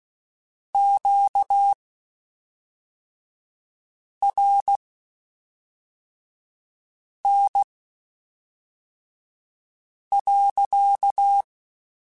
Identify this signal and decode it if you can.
Morse code, heard as a keyed tone, about 16 wpm, spaced as for 3 wpm, 800 Hz. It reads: QRN.
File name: Q Calls / Q Calls-mp3s / Q Calls sppc @5